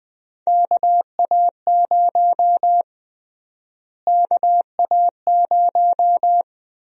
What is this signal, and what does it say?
Morse code, heard as a keyed tone, about 20 wpm, 700 Hz. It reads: KA0 KA0
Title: USA Call Sign Prefixes - Repeated 2x - 20wpm